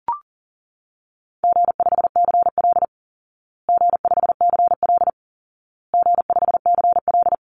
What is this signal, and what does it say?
Morse code, heard as a keyed tone, about 40 wpm, 700 Hz. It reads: G5CL G5CL G5CL
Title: Call Signs Repeated 2x - 40wpm